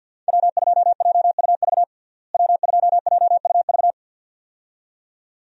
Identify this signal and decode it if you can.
Morse code, heard as a keyed tone, about 53 wpm, 700 Hz. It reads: W2JUV W2JUV
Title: USA Call Signs - 5 Characters - Repeated 2x - 50wpm